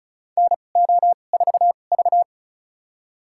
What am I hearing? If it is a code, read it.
Morse code, heard as a keyed tone, about 34 wpm, spaced as for 25 wpm, 700 Hz. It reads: NO4V
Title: USA Call Signs - 2x Character Spacing - 35wpm